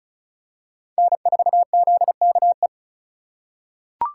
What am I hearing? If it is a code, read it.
Morse code, heard as a keyed tone, about 36 wpm, 700 Hz. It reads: N4ZKE
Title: USA Call Signs - 5 Characters - 2x Character Spacing - 35wpm